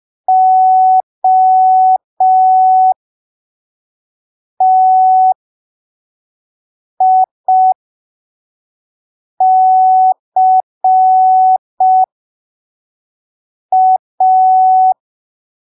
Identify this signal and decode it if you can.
Morse code, heard as a keyed tone, about 5 wpm, spaced as for 3 wpm, 750 Hz. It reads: OTICA
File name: arquivos / texto / texto 11 05ppm